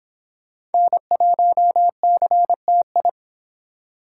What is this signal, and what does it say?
Morse code, heard as a keyed tone, about 26 wpm, 700 Hz. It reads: N1CTI